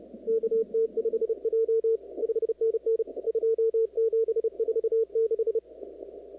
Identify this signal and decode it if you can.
Morse code, heard as a keyed tone, about 31 wpm, 450 Hz, with radio noise. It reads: KT5J 5NN 2746
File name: contests / 2019-cw-wpx-kt5j-k5pi- -2330 0-38